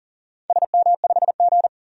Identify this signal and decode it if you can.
Morse code, heard as a keyed tone, about 40 wpm, 700 Hz. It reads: SM5G
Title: DXCC-100 Call Signs - 4 Characters - 40wpm